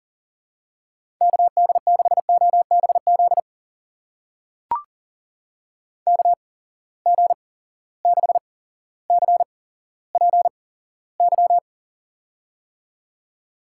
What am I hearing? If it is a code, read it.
Morse code, heard as a keyed tone, about 40 wpm, spaced as for 17 wpm, 700 Hz. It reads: KD6OBZ KG6CPY
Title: USA Call Signs - 6 Characters - 8x Character Spacing - 40wpm